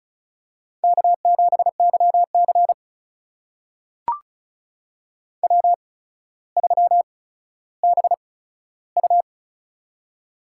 Morse code, heard as a keyed tone, about 36 wpm, spaced as for 15 wpm, 700 Hz. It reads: K7YC W3BU
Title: USA Call Signs - 4 Characters - 8x Character Spacing - 35wpm